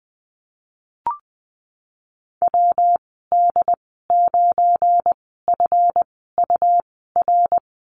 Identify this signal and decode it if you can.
Morse code, heard as a keyed tone, about 20 wpm, spaced as for 14 wpm, 700 Hz. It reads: WD9FUR
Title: 173 Lesson 28g - Number 9 - Call Signs 20wpm